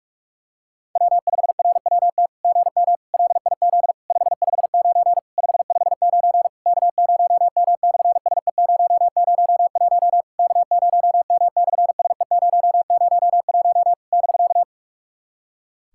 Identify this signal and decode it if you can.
Morse code, heard as a keyed tone, about 44 wpm, 660 Hz. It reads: W5RWT GM LIZ 559 559 K0M/SE001 K0M/SE001 <BK>
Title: SOTA QSO Practice 45wpm